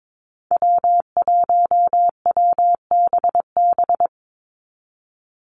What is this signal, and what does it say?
Morse code, heard as a keyed tone, about 22 wpm, 700 Hz. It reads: W1WBB
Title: Call Signs Repeated 2x - 22wpm